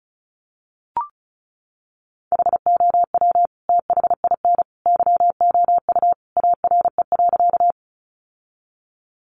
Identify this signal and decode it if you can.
Morse code, heard as a keyed tone, about 34 wpm, 700 Hz. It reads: HOW THIN YOU ARE.
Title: Sentences from Top 1200 Words 35wpm